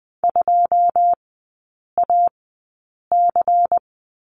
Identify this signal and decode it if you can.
Morse code, heard as a keyed tone, about 20 wpm, spaced as for 8 wpm, 690 Hz. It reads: 2AC
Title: 093 Lesson 16e - Letter c - Three Characters 20wpm